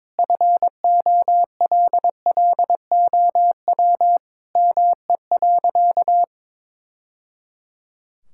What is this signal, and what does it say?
Morse code, heard as a keyed tone, about 22 wpm, 700 Hz. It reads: FOLLOW ME.
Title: Sentences from Top 1000 Words - Review Part 6 - 22wpm